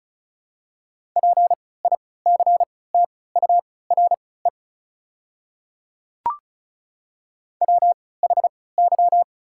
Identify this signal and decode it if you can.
Morse code, heard as a keyed tone, about 34 wpm, spaced as for 20 wpm, 690 Hz. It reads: PICTURE WHY
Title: 3x Character Spacing - Top 500 Words - 35wpm